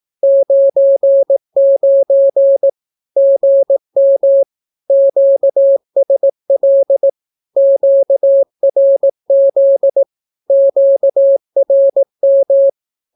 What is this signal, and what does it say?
Morse code, heard as a keyed tone, about 18 wpm, 550 Hz. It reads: 99 GM QSL QRZ QRM
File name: naukapodstaw / skroty / skroty